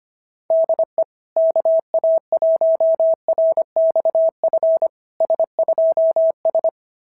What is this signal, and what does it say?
Morse code, heard as a keyed tone, about 25 wpm, 690 Hz. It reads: DE KA1RXF S2S
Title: SOTA QSO Practice 25wpm